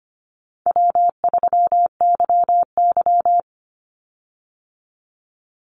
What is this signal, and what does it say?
Morse code, heard as a keyed tone, about 25 wpm, 700 Hz. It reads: W3YY
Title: Call Signs - 25wpm